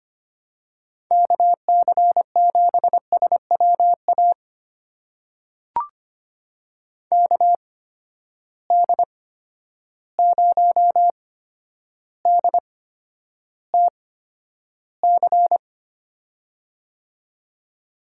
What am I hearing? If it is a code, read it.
Morse code, heard as a keyed tone, about 25 wpm, spaced as for 11 wpm, 700 Hz. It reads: KC7SWA KD0DTC
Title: USA Call Signs - 6 Characters - 8x Character Spacing - 25wpm